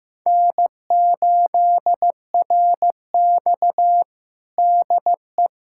Morse code, heard as a keyed tone, about 15 wpm, 700 Hz. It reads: N8RX DE
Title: USA Callsign DE Callsign - 4 Characters - 15wpm